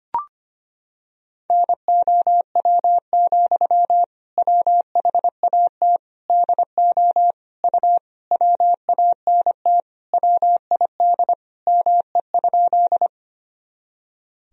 Morse code, heard as a keyed tone, about 25 wpm, 700 Hz. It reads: NOW, WHAT DO U WANT WID ME?
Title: Sentences from Top 200 Words and Abbreviations 25wpm